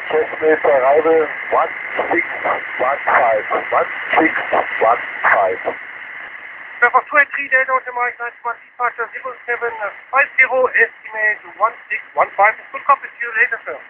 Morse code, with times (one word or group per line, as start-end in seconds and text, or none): none